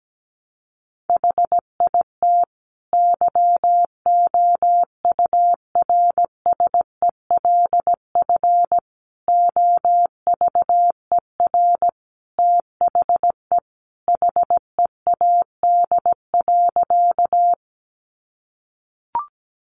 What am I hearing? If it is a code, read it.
Morse code, heard as a keyed tone, about 17 wpm, 700 Hz. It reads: HIT YOURSELF OVER THE HEAD.
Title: Sentences from Top 1200 Words 17wpm